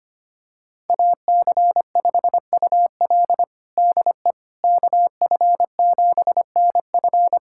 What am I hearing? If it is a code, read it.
Morse code, heard as a keyed tone, about 25 wpm, 700 Hz. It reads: AC5UL DE KF7NF
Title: USA Callsign DE Callsign 25wpm